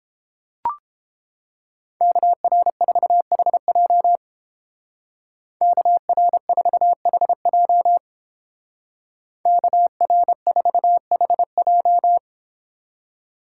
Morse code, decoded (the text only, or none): KR4HJ KR4HJ KR4HJ